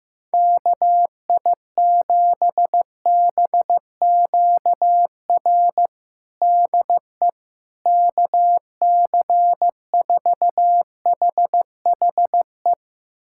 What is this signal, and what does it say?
Morse code, heard as a keyed tone, about 15 wpm, 700 Hz. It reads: KI7BQR DE KC4HHE
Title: USA Callsign DE Callsign 15wpm